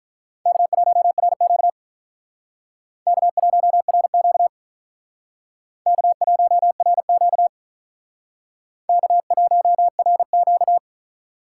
Morse code, heard as a keyed tone, about 44 wpm, 700 Hz. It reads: K1RQ K1RQ K1RQ K1RQ